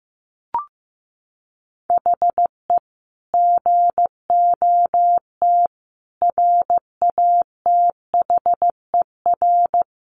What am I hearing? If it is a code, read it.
Morse code, heard as a keyed tone, about 15 wpm, 700 Hz. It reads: HE GOT RATHER
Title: Sentences from Top 700 Words 15wpm